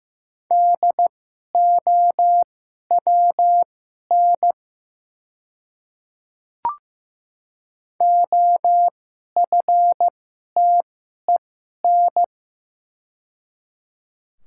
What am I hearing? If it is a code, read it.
Morse code, heard as a keyed tone, about 15 wpm, spaced as for 11 wpm, 700 Hz. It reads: DOWN OFTEN
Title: 2x Character Spacing - Top 300 Words - 15wpm